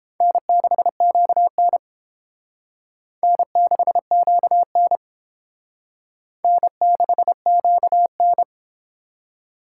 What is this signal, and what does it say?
Morse code, heard as a keyed tone, about 28 wpm, 700 Hz. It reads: N6QN N6QN N6QN